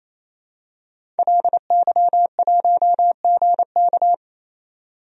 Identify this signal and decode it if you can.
Morse code, heard as a keyed tone, about 27 wpm, 700 Hz. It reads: LY1GK
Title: DXCC-100 Call Signs - 5 Characters - 28wpm